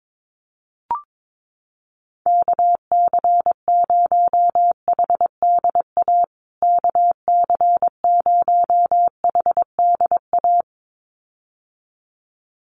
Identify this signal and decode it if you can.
Morse code, heard as a keyed tone, about 22 wpm, 700 Hz. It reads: KC0HDA KC0HDA
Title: USA Call Signs Repeated 2x 22wpm